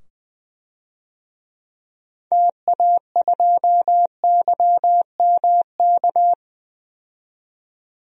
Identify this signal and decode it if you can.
Morse code, heard as a keyed tone, about 20 wpm, 700 Hz. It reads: TA2YMK
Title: DXCC-100 Call Signs - 6 Characters - 20wpm